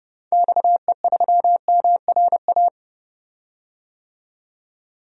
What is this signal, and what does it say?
Morse code, heard as a keyed tone, about 32 wpm, 700 Hz. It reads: XE3MRA
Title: DXCC-100 Call Signs - 6 Characters - 30wpm